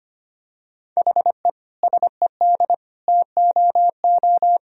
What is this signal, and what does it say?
Morse code, heard as a keyed tone, about 25 wpm, 700 Hz. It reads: HE SED TOO
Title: Sentences from Top 200 Words and Abbreviations 25wpm